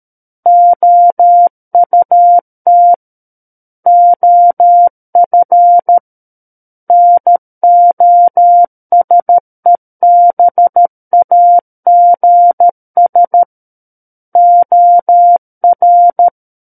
Morse code, heard as a keyed tone, about 13 wpm, 700 Hz. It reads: OUT OF NOSEBAGS OR
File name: Book One / 04-Chapter 4 - 13 WPM